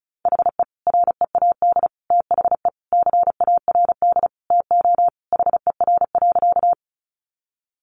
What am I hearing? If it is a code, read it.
Morse code, heard as a keyed tone, about 34 wpm, 700 Hz. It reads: HE READ THE CARD TO HER.
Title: Sentences from Top 600 Words 35wpm